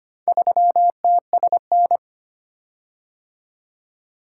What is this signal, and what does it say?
Morse code, heard as a keyed tone, about 25 wpm, 700 Hz. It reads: 3TSN
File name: practice / 25wpm / Four Letter-Number Combinations - 25wpm